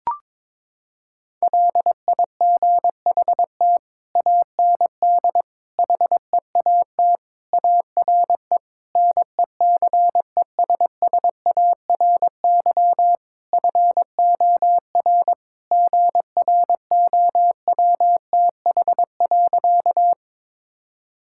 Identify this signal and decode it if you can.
Morse code, heard as a keyed tone, about 22 wpm, 700 Hz. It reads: LIGHT AND HEAT ARE NECESSARY FOR GROWTH.
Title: Sentences from Top 1000 Words - Review Part 5 - 22wpm